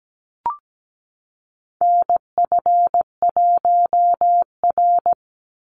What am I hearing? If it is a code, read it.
Morse code, heard as a keyed tone, about 17 wpm, 700 Hz. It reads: NF1R
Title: NA SOTA Chaser Callsigns 17wpm